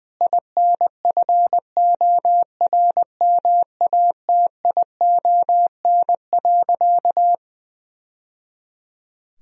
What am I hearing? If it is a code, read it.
Morse code, heard as a keyed tone, about 21 wpm, 700 Hz. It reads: INFORMATION.
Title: Sentences from Top 1000 Words - Review Part 3 - 20wpm